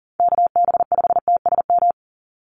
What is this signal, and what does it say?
Morse code, heard as a keyed tone, about 40 wpm, 690 Hz. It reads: KB5TSM
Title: USA Call Signs 40wpm